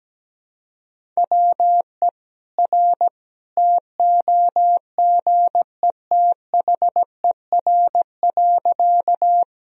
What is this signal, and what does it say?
Morse code, heard as a keyed tone, about 17 wpm, 700 Hz. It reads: WE R TOGETHER.
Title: Sentences from Top 300 Words and Abbreviations 17wpm